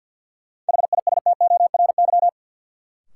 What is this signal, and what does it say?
Morse code, heard as a keyed tone, about 53 wpm, 700 Hz. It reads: HISTORY
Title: Sets of 3 Words - New Word Drill - 50wpm